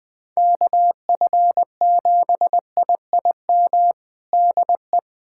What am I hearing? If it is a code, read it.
Morse code, heard as a keyed tone, about 21 wpm, 700 Hz. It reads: KF7IIM DE
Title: USA Callsign DE Callsign - 6 Characters - 20wpm